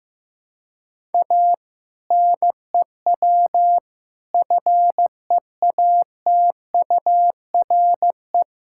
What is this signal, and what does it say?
Morse code, heard as a keyed tone, about 15 wpm, 700 Hz. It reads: A NEW FEATURE
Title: Sentences from Top 1000 Words - Review Part 5 - 15wpm